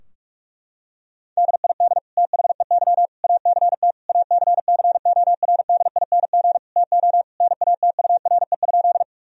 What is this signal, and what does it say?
Morse code, heard as a keyed tone, about 46 wpm, 700 Hz. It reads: DID THEY ACT ACCORDING TO NATURE?
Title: Sentences from Top 1000 Words - Review Part 4 - 45wpm